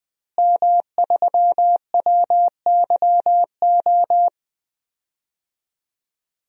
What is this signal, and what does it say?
Morse code, heard as a keyed tone, about 20 wpm, 700 Hz. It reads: M3WYO
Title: DXCC-100 Call Signs - 5 Characters - Repeated 3x - 20wpm